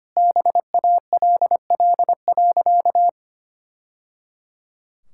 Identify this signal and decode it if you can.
Morse code, heard as a keyed tone, about 25 wpm, 700 Hz. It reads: BALL.
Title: Sentences from Top 1000 Words - Review Part 6 - 25wpm